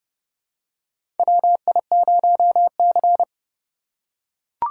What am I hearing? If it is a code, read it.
Morse code, heard as a keyed tone, about 30 wpm, 700 Hz. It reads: WI0C
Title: USA Call Signs - 4 Characters - 2x Character Spacing - 30wpm